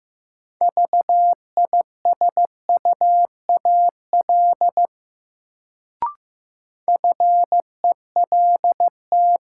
Morse code, heard as a keyed tone, about 15 wpm, 700 Hz. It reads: VISUAL FELT
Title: Sets of 5 Words Encore - New Word Drill - 15wpm